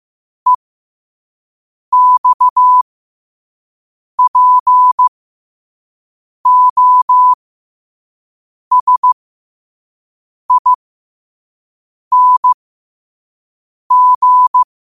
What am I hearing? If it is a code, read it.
Morse code, heard as a keyed tone, about 15 wpm, spaced as for 5 wpm, 1000 Hz. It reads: EXPOSING
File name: qotdfiles / 5wpm